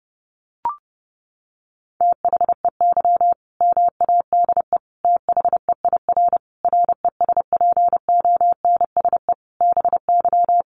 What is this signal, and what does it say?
Morse code, heard as a keyed tone, about 30 wpm, 700 Hz. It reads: THEY MADE THEIR RESPONSE BY
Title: Sentences from Top 1400 Words 30wpm